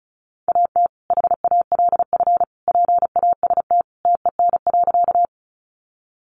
Morse code, heard as a keyed tone, about 34 wpm, 700 Hz. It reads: AT HALF PAST TEN.
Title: Sentences from Top 600 Words 35wpm